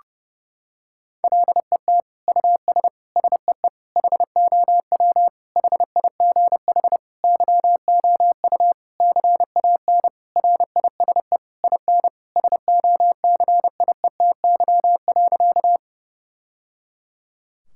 Morse code, heard as a keyed tone, about 30 wpm, 700 Hz. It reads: LET US SEE HOW HIGH YOU CAN RISE IN SOCIETY.
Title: Sentences from Top 1000 Words - Review Part 6 - 30wpm